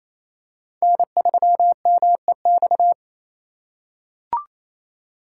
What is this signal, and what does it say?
Morse code, heard as a keyed tone, about 27 wpm, 700 Hz. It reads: N3MEX
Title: USA Call Signs - 3x Character Spacing - 28wpm